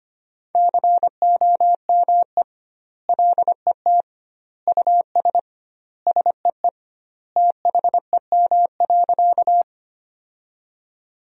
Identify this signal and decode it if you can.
Morse code, heard as a keyed tone, about 25 wpm, 700 Hz. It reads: COME LET US SEE THEM.
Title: 2x Word-Spacing - Sentences from Top 300 Words - 25wpm